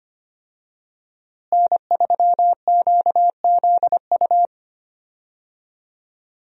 Morse code, heard as a keyed tone, about 25 wpm, 700 Hz. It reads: N3QZU
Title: USA Call Signs - 5 Characters - Repeated 3x - 25wpm